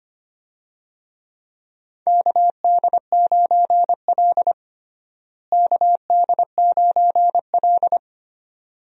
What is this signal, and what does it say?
Morse code, heard as a keyed tone, about 25 wpm, 700 Hz. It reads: KD9L KD9L
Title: USA Call Signs - 4 Characters - Repeated 3x - 25wpm